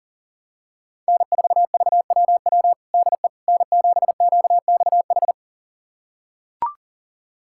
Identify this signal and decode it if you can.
Morse code, heard as a keyed tone, about 40 wpm, 700 Hz. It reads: N4VWW DE N7QXH
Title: USA Callsign DE Callsign - 5 Characters - 40wpm